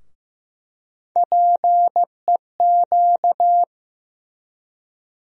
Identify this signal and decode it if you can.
Morse code, heard as a keyed tone, about 15 wpm, 700 Hz. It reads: PEQ
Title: USA Call Sign Postfixes - 3 Letters 15wpm